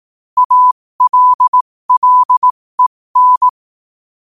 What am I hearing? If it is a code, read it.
Morse code, heard as a keyed tone, about 18 wpm, spaced as for 15 wpm, 1000 Hz. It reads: ALLEN
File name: qotdfiles / 15wpm